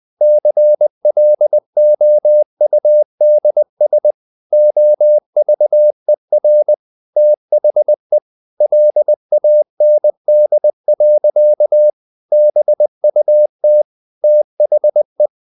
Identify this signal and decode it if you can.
Morse code, heard as a keyed tone, about 20 wpm, 600 Hz. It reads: CLOUDS OVER THE LAND. BUT THE
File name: the old man and the sea / the old man and the sea part 1 20wpm